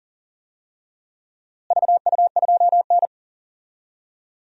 Morse code, heard as a keyed tone, about 40 wpm, 700 Hz. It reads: VU2N
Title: DXCC-100 Call Signs - 4 Characters - Repeated 3x - 40wpm